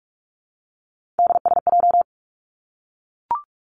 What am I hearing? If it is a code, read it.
Morse code, heard as a keyed tone, about 46 wpm, 700 Hz. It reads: DSJ
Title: Three Challenging Letter-Number Combinations 45wpm